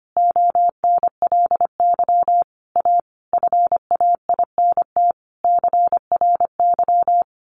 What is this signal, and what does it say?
Morse code, heard as a keyed tone, about 25 wpm, 700 Hz. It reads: ONLY A FAINT CRY
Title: A Sound Of Thunder 25wpm